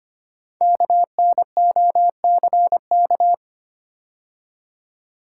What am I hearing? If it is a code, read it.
Morse code, heard as a keyed tone, about 25 wpm, 700 Hz. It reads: KNOCK
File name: 5 Letter Words - First 3 Letters Given - 25wpm